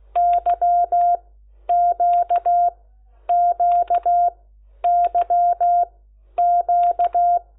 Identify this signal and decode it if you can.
Morse code, heard as a keyed tone, about 16 wpm, spaced as for 10 wpm, 700 Hz. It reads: YQQYQ
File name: Lecciones